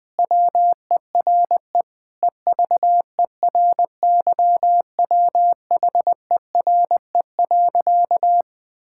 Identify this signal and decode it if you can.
Morse code, heard as a keyed tone, about 20 wpm, 700 Hz. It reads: WERE EVERYWHERE.
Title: Sentences with 8-Word Phrases - Spoken Beginning - 20wpm